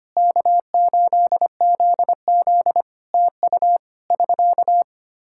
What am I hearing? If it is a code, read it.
Morse code, heard as a keyed tone, about 25 wpm, 700 Hz. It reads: K8ZZ TU <SK>
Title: Practice QSO - Exchange 4 - Ending - 25wpm